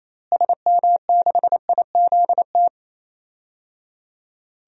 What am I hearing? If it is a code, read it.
Morse code, heard as a keyed tone, about 28 wpm, 700 Hz. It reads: SM6IZT
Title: DXCC-100 Call Signs - 6 Characters - Repeated 3x - 28wpm